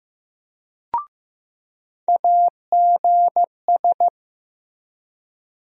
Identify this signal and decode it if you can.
Morse code, heard as a keyed tone, about 15 wpm, 700 Hz. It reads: AGS
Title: USA Call Sign Postfixes - 3 Letters 15wpm